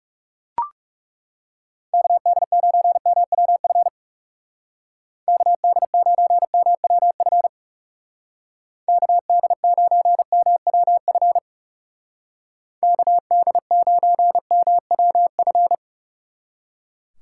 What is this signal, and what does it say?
Morse code, heard as a keyed tone, about 36 wpm, 700 Hz. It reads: KD9MWF KD9MWF KD9MWF KD9MWF
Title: Speed Racing - USA 2x3 Call Signs 30wpm